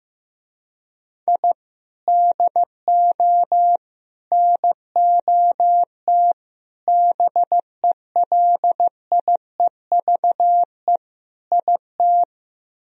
Morse code, heard as a keyed tone, about 15 wpm, 700 Hz. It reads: I DO NOT BELIEVE IT
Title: Speed-Racing - Sets of 5 Words 15wpm